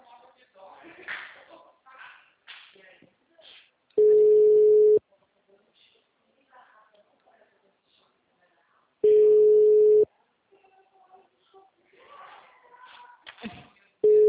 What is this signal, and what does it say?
Ringback tone (Europe, 425 Hz), with speech.